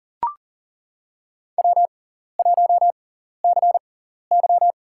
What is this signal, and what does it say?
Morse code, heard as a keyed tone, about 40 wpm, spaced as for 14 wpm, 700 Hz. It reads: W1CY
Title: USA Call Signs - 4 Characters - 6x Character Spacing - 40wpm